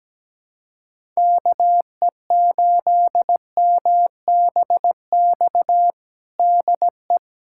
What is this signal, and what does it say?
Morse code, heard as a keyed tone, about 17 wpm, 700 Hz. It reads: KE8MBX DE
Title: USA Callsign DE Callsign 17wpm